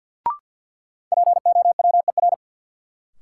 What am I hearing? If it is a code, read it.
Morse code, heard as a keyed tone, about 53 wpm, 700 Hz. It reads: POWER
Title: Sets of 4 Words - New Word Drill - 50wpm